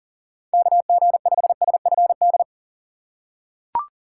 Morse code, heard as a keyed tone, about 39 wpm, 700 Hz. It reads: KG5SFD
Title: USA Call Signs - 6 Characters - 2x Character Spacing - 40wpm